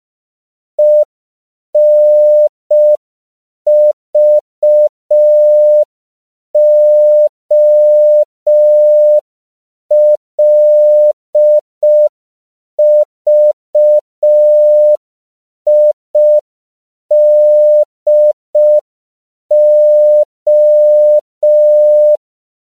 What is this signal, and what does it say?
Morse code, heard as a keyed tone, about 5 wpm, 600 Hz. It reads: ENVOLVIDO
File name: CW / 5PPM